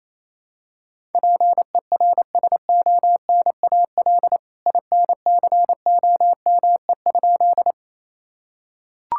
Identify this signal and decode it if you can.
Morse code, heard as a keyed tone, about 28 wpm, 700 Hz. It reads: PERSONAL INCOME?
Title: Sentences from Top 1000 Words - Review Part 5 - 28wpm